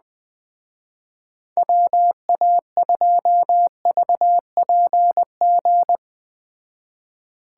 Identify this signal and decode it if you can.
Morse code, heard as a keyed tone, about 20 wpm, 700 Hz. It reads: WA2VPG